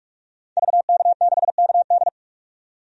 Morse code, heard as a keyed tone, about 46 wpm, 700 Hz. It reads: VK6KD
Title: DXCC-100 Call Signs - 5 Characters - 45wpm